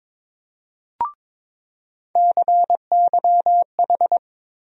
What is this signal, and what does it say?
Morse code, heard as a keyed tone, about 22 wpm, 700 Hz. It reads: CYH
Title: USA Call Sign Postfixes - 3 Letters 22wpm